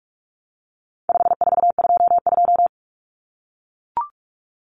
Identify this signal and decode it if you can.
Morse code, heard as a keyed tone, about 44 wpm, 700 Hz. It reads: 5422
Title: Numbers - 4-Digit 45wpm